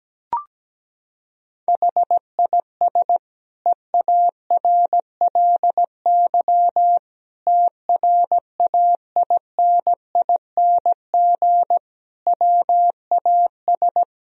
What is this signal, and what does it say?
Morse code, heard as a keyed tone, about 17 wpm, 700 Hz. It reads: HIS EARLY TRAINING WAS
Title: Sentences from Top 1000 Words - Review Part 4 - 17wpm